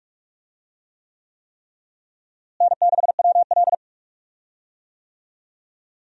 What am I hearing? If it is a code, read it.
Morse code, heard as a keyed tone, about 43 wpm, 700 Hz. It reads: N6WL
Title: USA Call Signs - 4 Characters - Repeated 3x - 45wpm